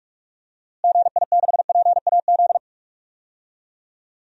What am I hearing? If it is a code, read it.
Morse code, heard as a keyed tone, about 46 wpm, 700 Hz. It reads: GI6PAZ
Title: DXCC-100 Call Signs - 6 Characters - Repeated 3x - 45wpm